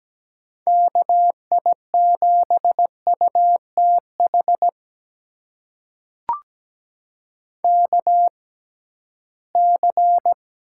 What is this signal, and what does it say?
Morse code, heard as a keyed tone, about 17 wpm, spaced as for 13 wpm, 700 Hz. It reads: KI7UTH KC
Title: USA Call Signs - 6 Characters - 6x Character Spacing - 17wpm